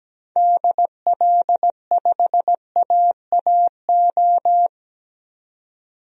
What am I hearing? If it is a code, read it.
Morse code, heard as a keyed tone, about 17 wpm, 700 Hz. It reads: DL5AAO
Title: DXCC-100 Call Signs - 6 Characters - 17wpm